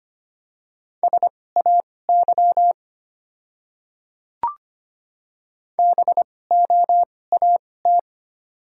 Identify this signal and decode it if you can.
Morse code, heard as a keyed tone, about 25 wpm, spaced as for 18 wpm, 690 Hz. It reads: SAY BOAT
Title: 2x Character Spacing - Top 500 Words - 25wpm